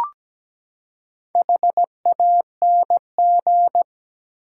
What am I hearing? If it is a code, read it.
Morse code, heard as a keyed tone, about 17 wpm, 700 Hz. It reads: HANG
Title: Four Letter Words - 17wpm